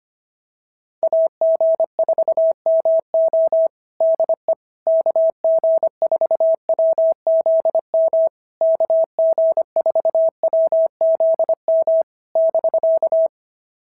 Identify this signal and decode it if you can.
Morse code, heard as a keyed tone, about 24 wpm, 650 Hz. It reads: AG4MO DE KG4WZM KG4WZM <BK>
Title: SOTA QSO Practice 25wpm